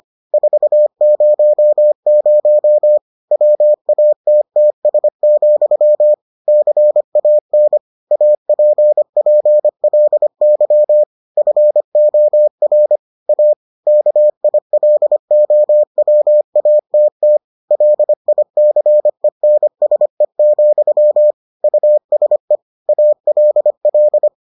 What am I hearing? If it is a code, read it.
Morse code, heard as a keyed tone, about 25 wpm, 600 Hz. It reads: 400 WATTS, CAN APPLY FOR A KILOWATT LICENSE, USE ALL